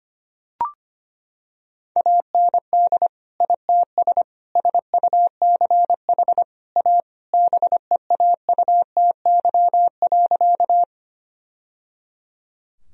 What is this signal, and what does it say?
Morse code, heard as a keyed tone, about 25 wpm, 700 Hz. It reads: AND ITS SUCH A BEAUTY.
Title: Sentences from Top 1000 Words - Review Part 2 - 25wpm